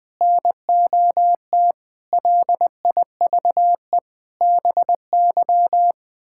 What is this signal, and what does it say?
Morse code, heard as a keyed tone, about 19 wpm, 700 Hz. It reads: NOT LIVE BY
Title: Sentences from Top 1600 Words 20wpm